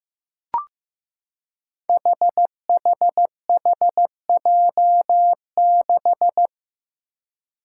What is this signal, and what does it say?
Morse code, heard as a keyed tone, about 15 wpm, 700 Hz. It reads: HHHJ6
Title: Five Challenging Letter-Number Combinations 15wpm